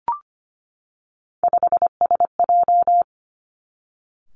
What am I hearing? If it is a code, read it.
Morse code, heard as a keyed tone, about 25 wpm, 700 Hz. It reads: 5SJ
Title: Three Challenging Letter-Number Combinations 25wpm